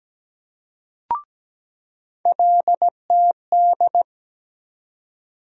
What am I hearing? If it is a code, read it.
Morse code, heard as a keyed tone, about 17 wpm, 700 Hz. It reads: LTD
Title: USA Call Sign Postfixes - 3 Letters 17wpm